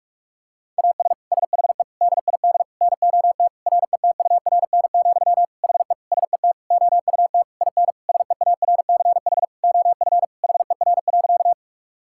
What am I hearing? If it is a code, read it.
Morse code, heard as a keyed tone, about 46 wpm, 700 Hz. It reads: AS SHE DID NOT RETURN, HE SET OUT IN SEARCH OF HER.
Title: Sentences from Top 1000 Words - Review Part 5 - 45wpm